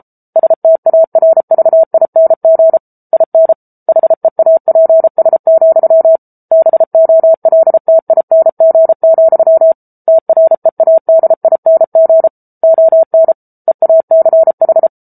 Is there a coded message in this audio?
Yes: Morse code from 0.4 to 14.9 s, heard as a keyed tone, about 34 wpm, 650 Hz. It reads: STARVING IN HEAPS, BOLTING, TREADING ON EACH